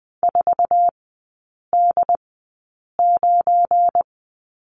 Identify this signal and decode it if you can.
Morse code, heard as a keyed tone, about 20 wpm, spaced as for 8 wpm, 700 Hz. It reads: 4D9
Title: 191 Lesson 31e - Letter v - Three Characters 20wpm